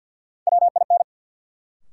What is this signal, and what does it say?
Morse code, heard as a keyed tone, about 53 wpm, 700 Hz. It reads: WIN
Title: Sentences from Top 1500 Words - New Word Drill - 50wpm